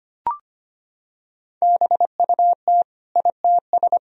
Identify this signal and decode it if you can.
Morse code, heard as a keyed tone, about 25 wpm, 700 Hz. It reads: BUT ITS
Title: Sentences from Top 1000 Words - Review Part 4 - 25wpm